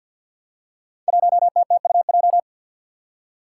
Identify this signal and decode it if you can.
Morse code, heard as a keyed tone, about 50 wpm, 700 Hz. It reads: 1TTUJ
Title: Five Challenging Letter-Number Combinations 50wpm